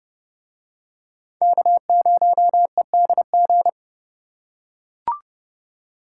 Morse code, heard as a keyed tone, about 29 wpm, 700 Hz. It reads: K0EDG